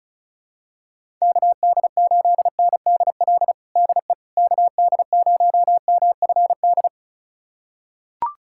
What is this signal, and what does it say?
Morse code, heard as a keyed tone, about 36 wpm, 700 Hz. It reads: KD8NDL DE KD0MFD
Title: USA Callsign DE Callsign 35wpm